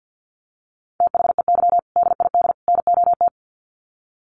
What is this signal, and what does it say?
Morse code, heard as a keyed tone, about 49 wpm, 700 Hz. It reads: THEY DID NOT
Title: Sets of 3 Words 50wpm